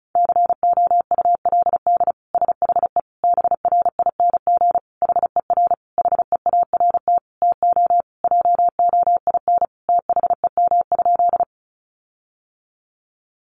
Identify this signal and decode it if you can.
Morse code, heard as a keyed tone, about 34 wpm, 700 Hz. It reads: COULD SHE BRING HER HEART TO JOIN THEM?
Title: Sentences from Top 1400 Words 35wpm